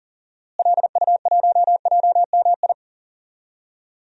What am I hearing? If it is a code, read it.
Morse code, heard as a keyed tone, about 40 wpm, 700 Hz. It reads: LU1JMI
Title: DXCC-100 Call Signs - 6 Characters - Repeated 2x - 40wpm